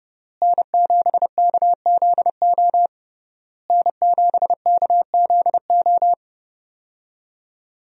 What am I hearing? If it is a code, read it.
Morse code, heard as a keyed tone, about 32 wpm, 700 Hz. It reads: N7KZO N7KZO
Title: USA Call Signs - 5 Characters - Repeated 3x - 30wpm